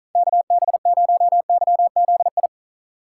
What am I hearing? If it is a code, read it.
Morse code, heard as a keyed tone, about 43 wpm, 700 Hz. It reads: KB0YZI
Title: Speed Racing - USA 2x3 Call Signs 35wpm